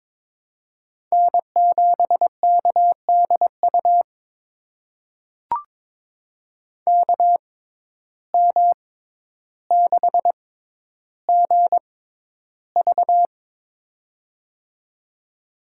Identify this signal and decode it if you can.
Morse code, heard as a keyed tone, about 22 wpm, spaced as for 11 wpm, 700 Hz. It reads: N7KDU KM6GV
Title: USA Call Signs - 5 Characters - 6x Character Spacing - 22wpm